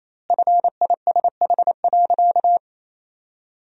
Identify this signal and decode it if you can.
Morse code, heard as a keyed tone, about 28 wpm, 700 Hz. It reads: FISH.
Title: 2x Word-Spacing - Sentences from Top 300 Words - 28wpm